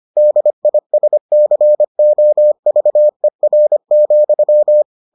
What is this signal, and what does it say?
Morse code, heard as a keyed tone, about 25 wpm, 600 Hz. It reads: DISCOVER,